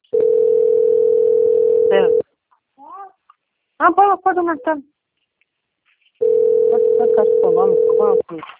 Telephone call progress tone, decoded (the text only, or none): ringback tone